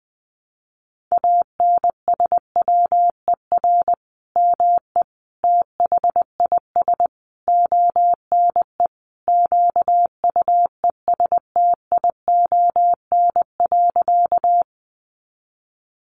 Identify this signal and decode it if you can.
Morse code, heard as a keyed tone, about 20 wpm, 700 Hz. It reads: ANSWER ME THIS ONE QUESTION.
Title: Sentences from Top 400 Words 20wpm